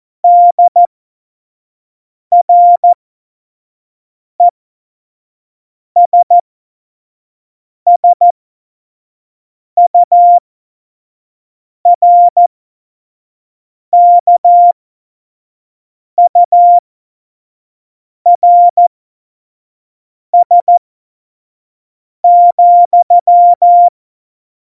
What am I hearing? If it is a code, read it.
Morse code, heard as a keyed tone, about 14 wpm, spaced as for 5 wpm, 700 Hz. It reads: DRESSURKURS,
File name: Morsekurs / Witze / Witz 04 August 2020, 25 F70